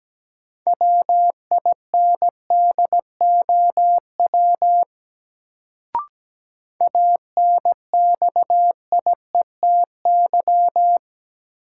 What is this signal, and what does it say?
Morse code, heard as a keyed tone, about 17 wpm, 700 Hz. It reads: WINDOW ANXIETY
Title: Sentences from Top 1600 Words - New Word Drill - 17wpm